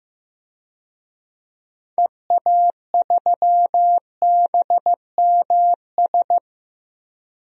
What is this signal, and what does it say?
Morse code, heard as a keyed tone, about 15 wpm, 700 Hz. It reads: EA3BMS
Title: DXCC-100 Call Signs - 6 Characters - Repeated 2x - 15wpm